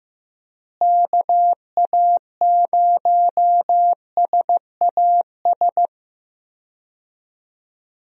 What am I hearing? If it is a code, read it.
Morse code, heard as a keyed tone, about 14 wpm, 700 Hz. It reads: KA0SAS
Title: USA Call Signs - 6 Characters - Repeated 3x - 15wpm